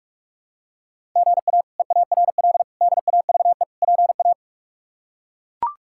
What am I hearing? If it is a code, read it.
Morse code, heard as a keyed tone, about 42 wpm, 700 Hz. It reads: GA EARL DAVE PA
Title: K1USN SST QSO Practice - 45wpm